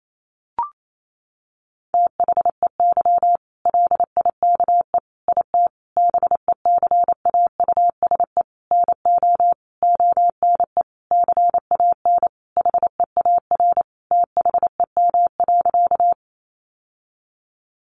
Morse code, heard as a keyed tone, about 28 wpm, 700 Hz. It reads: THEY LIKE IT BECAUSE NO ONE CAN HEAR THEM.
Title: Sentences from Top 600 Words 28wpm